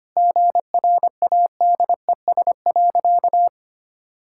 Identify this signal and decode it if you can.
Morse code, heard as a keyed tone, about 25 wpm, 700 Hz. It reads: GRADES.
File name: practice / 25wpm / Sentences with 3-Word Phrases - 25wpm